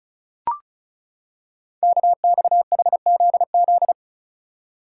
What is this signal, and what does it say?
Morse code, heard as a keyed tone, about 36 wpm, 700 Hz. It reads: KXHZZ
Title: Code Groups - 5 Letters - 35wpm